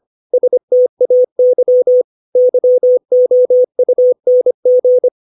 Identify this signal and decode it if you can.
Morse code, heard as a keyed tone, about 25 wpm, 500 Hz. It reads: STAY YOUNG
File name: 632 words / 632 Words 25WPM